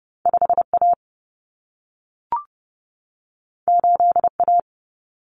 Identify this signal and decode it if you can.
Morse code, heard as a keyed tone, about 30 wpm, 700 Hz. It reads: 5A 8A
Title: Common Cut Numbers - 2-Digit 30wpm